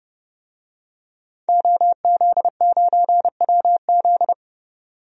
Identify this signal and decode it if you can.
Morse code, heard as a keyed tone, about 32 wpm, 700 Hz. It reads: OZ9WZ
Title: DXCC-100 Call Signs - 5 Characters - 30wpm